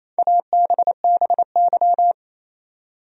Morse code, heard as a keyed tone, about 28 wpm, 700 Hz. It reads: ABBY
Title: USA Common Names - 28wpm